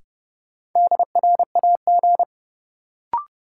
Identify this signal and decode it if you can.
Morse code, heard as a keyed tone, about 30 wpm, 700 Hz. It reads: DRAG
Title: Sentences from Top 1500 Words - New Word Drill - 30wpm